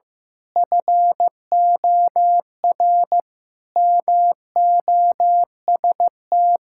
Morse code, heard as a keyed tone, about 15 wpm, 700 Hz. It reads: FOR MOST
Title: Pangram Challenge v2 - 15wpm